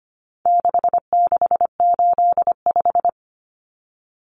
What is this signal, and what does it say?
Morse code, heard as a keyed tone, about 25 wpm, 700 Hz. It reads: 6685
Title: Common Cut Numbers - 4-Digit 25wpm